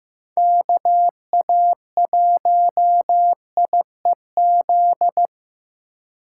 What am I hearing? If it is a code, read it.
Morse code, heard as a keyed tone, about 15 wpm, 700 Hz. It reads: KA1IEZ
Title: USA Call Signs - 4x Character Spacing - 15wpm